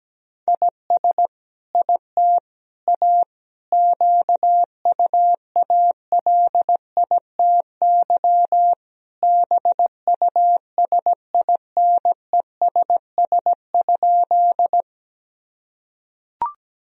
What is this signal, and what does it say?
Morse code, heard as a keyed tone, about 17 wpm, 700 Hz. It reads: IS IT A QUALITY BUSINESS?
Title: Sentences from Top 1000 Words - Review Part 4 - 17wpm